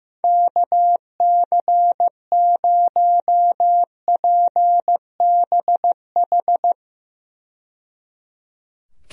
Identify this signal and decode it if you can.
Morse code, heard as a keyed tone, about 15 wpm, 700 Hz. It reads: KC0PBH
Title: USA Call Signs - 6 Characters - 15wpm